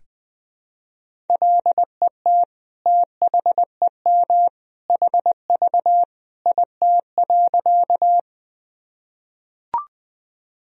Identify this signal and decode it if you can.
Morse code, heard as a keyed tone, about 21 wpm, 700 Hz. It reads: LET THEM HV IT.
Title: Sentences from Top 300 Words and Abbreviations 20wpm